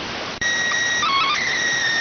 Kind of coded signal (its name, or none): SSTV (slow-scan television)